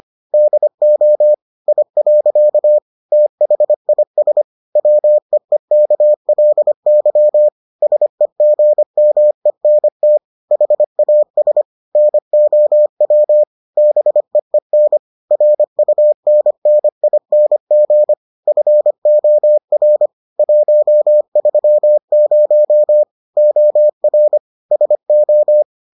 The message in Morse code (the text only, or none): DO I. THIS WEEKLY SEGMENT HAS NOW BEEN RUNNING FOR 130 OR SO